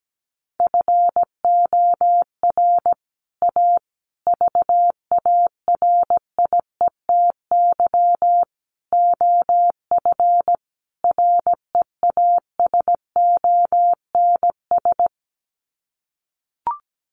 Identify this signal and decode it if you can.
Morse code, heard as a keyed tone, about 17 wpm, 700 Hz. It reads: FOR A VARIETY OF REASONS
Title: Sets of 5 Words Encore 17wpm